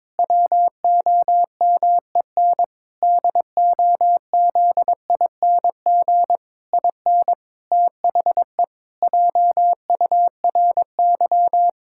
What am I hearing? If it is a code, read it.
Morse code, heard as a keyed tone, about 22 wpm, 700 Hz. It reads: WOMEN DOZING IN THE JURY
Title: Pangram Challenge v2 - 22wpm